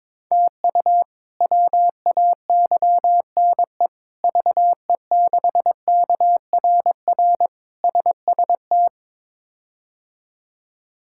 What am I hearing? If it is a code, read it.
Morse code, heard as a keyed tone, about 22 wpm, 700 Hz. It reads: TU WAYNE VE6KRR SST
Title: K1USN SST QSO Practice - 22wpm